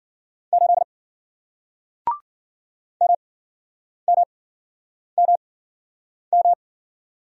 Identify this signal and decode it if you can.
Morse code, heard as a keyed tone, about 53 wpm, spaced as for 9 wpm, 700 Hz. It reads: 8 MMMM